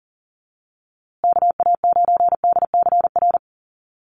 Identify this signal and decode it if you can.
Morse code, heard as a keyed tone, about 40 wpm, 700 Hz. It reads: KA9DCR
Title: USA Call Signs Repeated 2x 40wpm